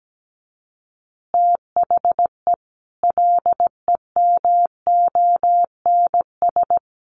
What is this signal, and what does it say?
Morse code, heard as a keyed tone, about 17 wpm, 700 Hz. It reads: THE LEMONS
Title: Additional Easy Phrases 17wpm